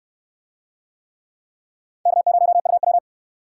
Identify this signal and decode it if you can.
Morse code, heard as a keyed tone, about 66 wpm, 700 Hz. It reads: K0UW